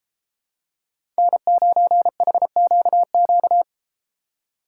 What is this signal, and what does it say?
Morse code, heard as a keyed tone, about 33 wpm, 700 Hz. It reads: N9HQQ